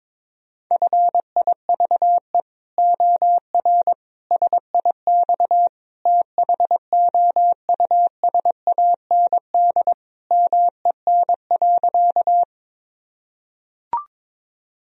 Morse code, heard as a keyed tone, about 22 wpm, 700 Hz. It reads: FIVE OR SIX THOUSAND MEN.
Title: Sentences from Top 1000 Words - Review Part 3 - 22wpm